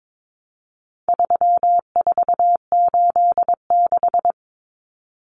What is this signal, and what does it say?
Morse code, heard as a keyed tone, about 22 wpm, 700 Hz. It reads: 3486
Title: Common Cut Numbers - 4-Digit 22wpm